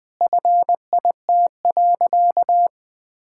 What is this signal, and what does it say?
Morse code, heard as a keyed tone, about 21 wpm, 700 Hz. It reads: FIT.
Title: Sentences from Top 1000 Words - Review Part 5 - 20wpm